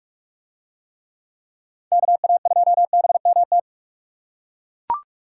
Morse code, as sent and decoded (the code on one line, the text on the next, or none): -.- .- ..--- -... -- -
KA2BMT